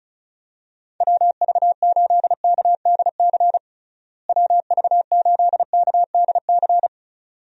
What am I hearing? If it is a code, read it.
Morse code, heard as a keyed tone, about 36 wpm, 700 Hz. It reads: WV8KDC WV8KDC
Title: USA Call Signs - 6 Characters - Repeated 2x - 35wpm